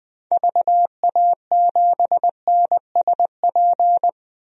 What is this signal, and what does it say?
Morse code, heard as a keyed tone, about 21 wpm, 700 Hz. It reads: VA7NSP
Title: DXCC-100 Call Signs - 6 Characters - 20wpm